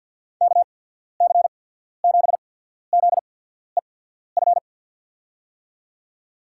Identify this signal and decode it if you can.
Morse code, heard as a keyed tone, about 53 wpm, spaced as for 14 wpm, 690 Hz. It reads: KC7ZEF
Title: USA Call Signs - 6 Characters - 8x Character Spacing - 50wpm